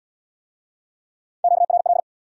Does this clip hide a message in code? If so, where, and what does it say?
Yes: Morse code from 1.4 to 2.0 s, heard as a keyed tone, about 70 wpm, 700 Hz. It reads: QRL